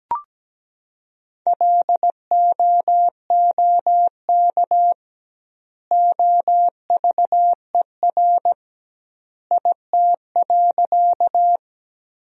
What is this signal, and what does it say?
Morse code, heard as a keyed tone, about 17 wpm, 700 Hz. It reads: LOOK OVER IT.
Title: 2x Word-Spacing - Sentences from Top 100 Words - 17wpm